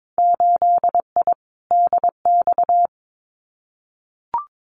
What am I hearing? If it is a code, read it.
Morse code, heard as a keyed tone, about 22 wpm, 700 Hz. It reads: 8I DX
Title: Winter Field Day Exchanges Encore 22wpm